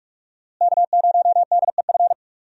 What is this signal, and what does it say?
Morse code, heard as a keyed tone, about 46 wpm, 700 Hz. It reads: K0DEF
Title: USA Call Signs - 5 Characters - Repeated 2x - 45wpm